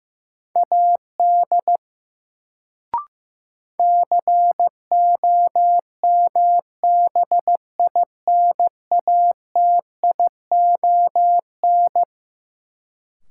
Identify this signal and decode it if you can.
Morse code, heard as a keyed tone, about 15 wpm, 700 Hz. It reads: AD COMBINATION
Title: Sentences from Top 1300 Words - New Word Drill Alt - 15wpm